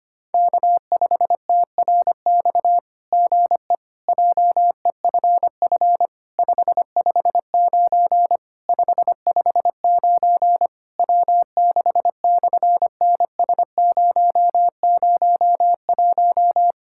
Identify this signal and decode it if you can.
Morse code, heard as a keyed tone, about 24 wpm, 700 Hz. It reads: K5TRX GE JEFF 559 559 W6/NS001
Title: SOTA QSO Practice 25wpm